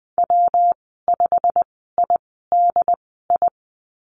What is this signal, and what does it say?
Morse code, heard as a keyed tone, about 20 wpm, spaced as for 14 wpm, 700 Hz. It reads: W5IDI
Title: 161 Lesson 26g - Letter g - Call Signs 20wpm